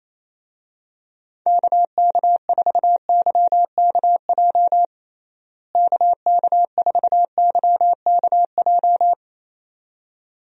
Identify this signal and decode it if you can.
Morse code, heard as a keyed tone, about 27 wpm, 700 Hz. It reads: KK4YKJ KK4YKJ
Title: USA Call Signs - 6 Characters - Repeated 2x - 28wpm